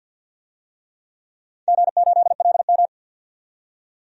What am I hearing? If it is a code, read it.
Morse code, heard as a keyed tone, about 46 wpm, 700 Hz. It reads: G8LM
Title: DXCC-100 Call Signs - 4 Characters - 50wpm